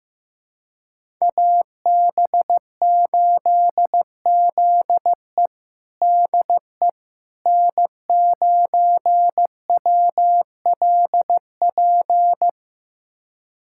Morse code, heard as a keyed tone, about 15 wpm, 700 Hz. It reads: AB8ZE DE N9WLP
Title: USA Callsign DE Callsign - 5 Characters - 15wpm